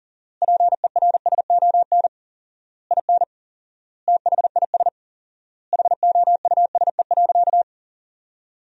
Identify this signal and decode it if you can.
Morse code, heard as a keyed tone, about 40 wpm, 700 Hz. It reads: PERSON IN THIS HOUSE.
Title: 4x Word-Spacing - Sentences from Top 400 Words - 40wpm